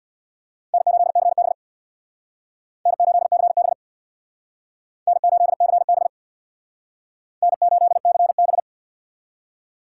Morse code, heard as a keyed tone, about 62 wpm, 700 Hz. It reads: N8CB N8CB N8CB N8CB
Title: Speed Racing - USA 1x2 Call Signs 50wpm